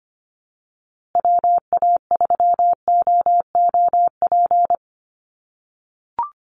Morse code, heard as a keyed tone, about 25 wpm, 700 Hz. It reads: WA3OOP
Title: USA Call Signs Repeated 3x 25wpm